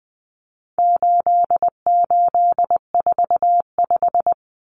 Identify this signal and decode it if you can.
Morse code, heard as a keyed tone, about 20 wpm, 700 Hz. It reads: 8845
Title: Numbers - 4-Digit 20wpm